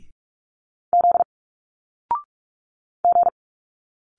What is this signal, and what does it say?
Morse code, heard as a keyed tone, about 44 wpm, 700 Hz. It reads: Z G
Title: Single Letters 45wpm